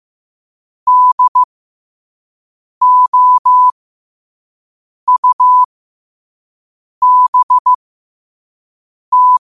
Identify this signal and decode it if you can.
Morse code, heard as a keyed tone, about 15 wpm, spaced as for 5 wpm, 1000 Hz. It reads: DOUBT
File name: qotdfiles / 5wpm